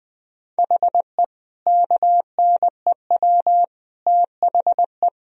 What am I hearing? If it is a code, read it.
Morse code, heard as a keyed tone, about 20 wpm, 700 Hz. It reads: HE KNEW THE
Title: Sentences from Top 1000 Words - Review Part 1 - 20wpm